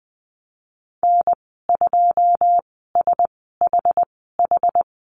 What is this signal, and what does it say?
Morse code, heard as a keyed tone, about 19 wpm, spaced as for 14 wpm, 700 Hz. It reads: N2SHH
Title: 082 Lesson 14g - Number 2 - Call Signs 20wpm